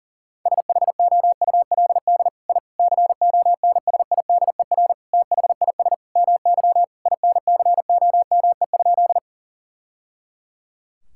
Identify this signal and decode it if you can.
Morse code, heard as a keyed tone, about 40 wpm, 700 Hz. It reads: SHOULD I CONSIDER THIS MY INCOME?
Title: Sentences from Top 1000 Words - Review Part 5 - 40wpm